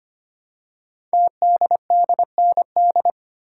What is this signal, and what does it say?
Morse code, heard as a keyed tone, about 25 wpm, 700 Hz. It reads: TDDND
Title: Code Groups - 5 Letters - 25wpm